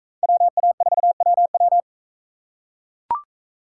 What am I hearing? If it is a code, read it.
Morse code, heard as a keyed tone, about 41 wpm, 700 Hz. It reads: WA4WW